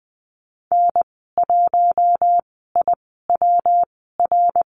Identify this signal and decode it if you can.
Morse code, heard as a keyed tone, about 20 wpm, spaced as for 15 wpm, 700 Hz. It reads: N1IWR
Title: 167 Lesson 27g - Number 7 - Call Signs 20wpm